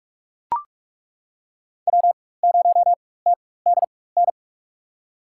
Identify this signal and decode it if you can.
Morse code, heard as a keyed tone, about 42 wpm, spaced as for 20 wpm, 690 Hz. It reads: W0TDN
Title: USA Call Signs - 5 Characters - 4x Character Spacing - 45wpm